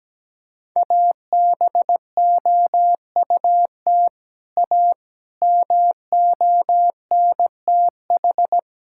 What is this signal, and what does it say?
Morse code, heard as a keyed tone, about 17 wpm, 700 Hz. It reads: ABOUT A MONTH
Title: Sentences from Top 1000 Words - Review Part 5 - 17wpm